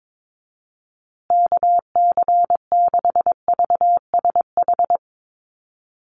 Morse code, heard as a keyed tone, about 22 wpm, 700 Hz. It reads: KC6VSH
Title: USA Call Signs Repeated 2x 22wpm